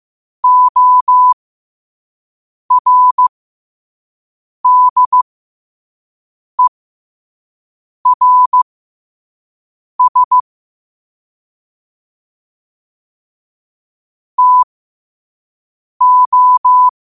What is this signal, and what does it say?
Morse code, heard as a keyed tone, about 15 wpm, spaced as for 5 wpm, 1000 Hz. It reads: ORDERS TO
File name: qotdfiles / 5wpm